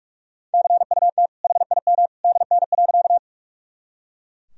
Morse code, heard as a keyed tone, about 43 wpm, 700 Hz. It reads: CUT HIM DN.
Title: Sentences from Top 300 Words and Abbreviations 45wpm